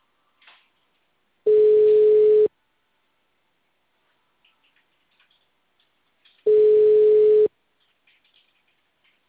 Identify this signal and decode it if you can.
Ringback tone (Europe, 425 Hz).